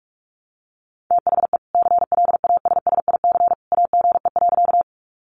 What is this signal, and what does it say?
Morse code, heard as a keyed tone, about 44 wpm, 700 Hz. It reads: THE CLASSIC AGE.
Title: Sentences from Top 1000 Words 45wpm